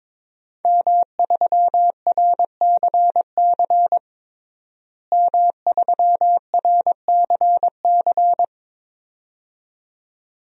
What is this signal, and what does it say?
Morse code, heard as a keyed tone, about 21 wpm, 700 Hz. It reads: M3RCC M3RCC
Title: DXCC-100 Call Signs - 5 Characters - Repeated 2x - 22wpm